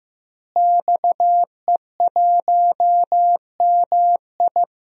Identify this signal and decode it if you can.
Morse code, heard as a keyed tone, about 15 wpm, 700 Hz. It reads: XE1MI
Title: DXCC-100 Call Signs - 5 Characters - Repeated 2x - 15wpm